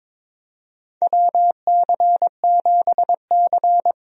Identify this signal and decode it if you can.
Morse code, heard as a keyed tone, about 22 wpm, 700 Hz. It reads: WC7C
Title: USA Call Signs - 4 Characters - Repeated 3x - 22wpm